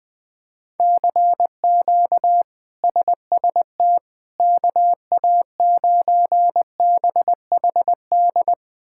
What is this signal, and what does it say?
Morse code, heard as a keyed tone, about 21 wpm, 700 Hz. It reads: CQ SST KA9BHD
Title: K1USN SST QSO Practice - 20wpm